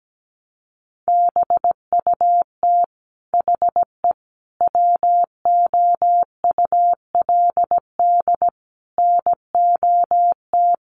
Morse code, heard as a keyed tone, about 17 wpm, 700 Hz. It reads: BUT HE WOULD NOT
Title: Sentences from Top 900 Words 17wpm